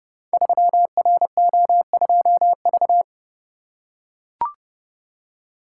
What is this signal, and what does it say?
Morse code, heard as a keyed tone, about 30 wpm, 700 Hz. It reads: 3RO2V
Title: Code Groups - 5 Letter:Numbers - 30wpm